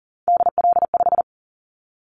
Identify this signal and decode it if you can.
Morse code, heard as a keyed tone, about 40 wpm, 700 Hz. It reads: DL5
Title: Callsign Prefixes for Top 25 DXCC - 40wpm